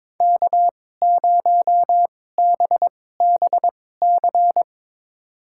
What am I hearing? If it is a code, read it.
Morse code, heard as a keyed tone, about 22 wpm, spaced as for 16 wpm, 700 Hz. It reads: K0BBC
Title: USA Call Signs - 5 Characters - 2x Character Spacing - 22wpm